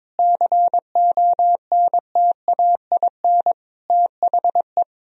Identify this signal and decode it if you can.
Morse code, heard as a keyed tone, about 22 wpm, 700 Hz. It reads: CONTAIN THE
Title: Sentences from Top 1000 Words - Review Part 2 - 22wpm